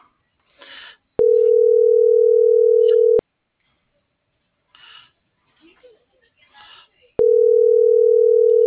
Ringback tone (North America), 440 + 480 Hz.